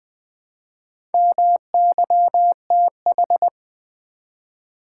Morse code, heard as a keyed tone, about 20 wpm, 700 Hz. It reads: MYTH
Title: Build A Word - 1 Letter At A Time - 3 to 5-Letter Words - 20wpm